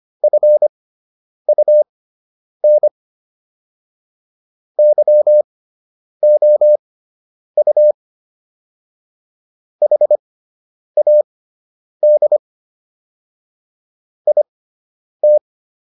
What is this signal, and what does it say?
Morse code, heard as a keyed tone, about 25 wpm, spaced as for 9 wpm, 600 Hz. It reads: FUN YOU HAD IT